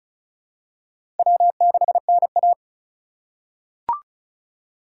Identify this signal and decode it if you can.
Morse code, heard as a keyed tone, about 34 wpm, 700 Hz. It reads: W6NA
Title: USA Call Signs - 4 Characters - 6x Character Spacing - 35wpm